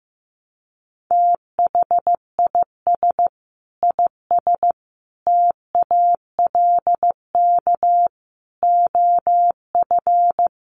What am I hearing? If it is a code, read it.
Morse code, heard as a keyed tone, about 15 wpm, 700 Hz. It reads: THIS IS TALK OF
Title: Sets of 5 Words 15wpm